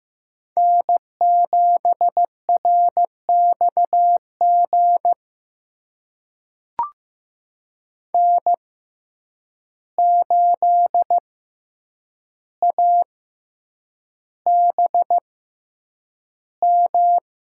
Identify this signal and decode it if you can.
Morse code, heard as a keyed tone, about 15 wpm, spaced as for 8 wpm, 700 Hz. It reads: N7RXG N8ABM
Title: USA Call Signs - 5 Characters - 6x Character Spacing - 15wpm